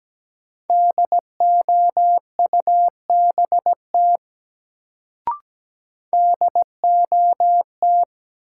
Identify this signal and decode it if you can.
Morse code, heard as a keyed tone, about 17 wpm, 700 Hz. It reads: DOUBT DOT
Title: Sentences from Top 1500 Words - New Word Drill Alt - 17wpm